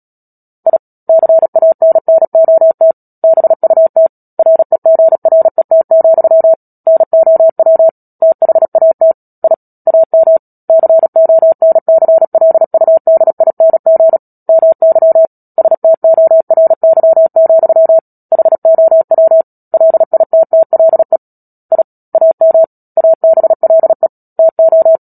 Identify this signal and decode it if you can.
Morse code, heard as a keyed tone, about 37 wpm, 650 Hz. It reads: I CANNOT BUT REGRET, NOW THAT I AM CONCLUDING MY STORY, HOW LITTLE I AM ABLE TO
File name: Book Two / 10-Chapter 10 - Epilogue - 36 WPM